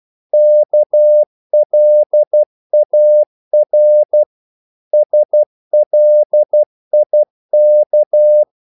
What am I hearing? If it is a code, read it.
Morse code, heard as a keyed tone, about 12 wpm, 600 Hz. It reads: KLAR SLIK
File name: Korte ord DK 12wpm